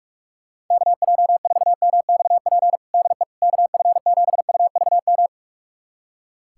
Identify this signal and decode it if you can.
Morse code, heard as a keyed tone, about 46 wpm, 700 Hz. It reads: KJ4MXP DE KF7UVM
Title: USA Callsign DE Callsign 45wpm